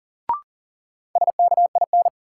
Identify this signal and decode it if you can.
Morse code, heard as a keyed tone, about 40 wpm, 700 Hz. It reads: SKIN
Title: Sentences from Top 1300 Words - New Word Drill Alt - 40wpm